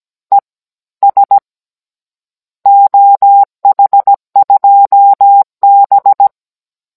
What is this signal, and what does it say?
Morse code, heard as a keyed tone, about 17 wpm, 800 Hz. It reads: E S OH2B
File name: Cours-NR18-16mm